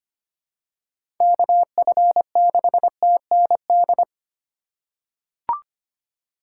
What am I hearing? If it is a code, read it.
Morse code, heard as a keyed tone, about 25 wpm, 700 Hz. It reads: KF6TND
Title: USA Call Signs - 6 Characters - 3x Character Spacing - 25wpm